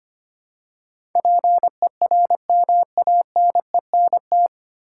Morse code, heard as a keyed tone, about 25 wpm, 700 Hz. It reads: PERMANENT
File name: Nine Letter Words - 25wpm